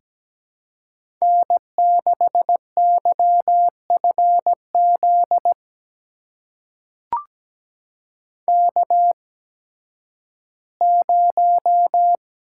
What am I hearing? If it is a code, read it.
Morse code, heard as a keyed tone, about 17 wpm, spaced as for 11 wpm, 700 Hz. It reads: N6YFZ K0
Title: USA Call Signs - 5 Characters - 8x Character Spacing - 17wpm